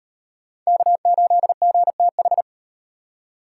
Morse code, heard as a keyed tone, about 38 wpm, 700 Hz. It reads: K8GTH